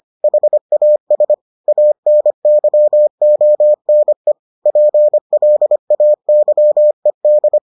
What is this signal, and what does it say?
Morse code, heard as a keyed tone, about 25 wpm, 600 Hz. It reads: HAS ANYONE PLAYED